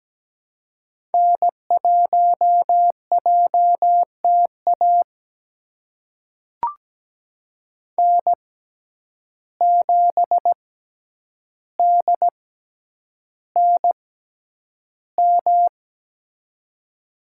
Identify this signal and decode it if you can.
Morse code, heard as a keyed tone, about 17 wpm, spaced as for 9 wpm, 700 Hz. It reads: N1JTA N7DNM
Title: USA Call Signs - 5 Characters - 6x Character Spacing - 17wpm